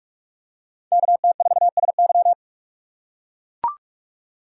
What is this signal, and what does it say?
Morse code, heard as a keyed tone, about 44 wpm, 700 Hz. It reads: KT4SY